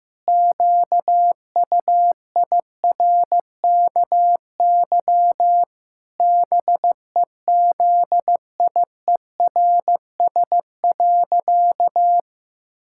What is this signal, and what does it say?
Morse code, heard as a keyed tone, about 15 wpm, 700 Hz. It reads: QUIRKY BEZIERS.
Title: Pangram Challenge v2 - 15wpm